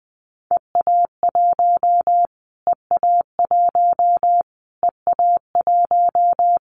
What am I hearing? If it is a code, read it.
Morse code, heard as a keyed tone, about 20 wpm, 700 Hz. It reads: EA1 EA1 EA1
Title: Callsign Prefixes for Top 25 DXCC - Repeated 3x - 20wpm